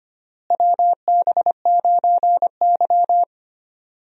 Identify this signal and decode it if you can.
Morse code, heard as a keyed tone, about 25 wpm, 700 Hz. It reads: WB9Y
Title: USA Call Signs - 4 Characters - Repeated 3x - 25wpm